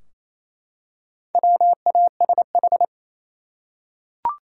Morse code, heard as a keyed tone, about 28 wpm, 700 Hz. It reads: WASH
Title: Four Letter Words - 28wpm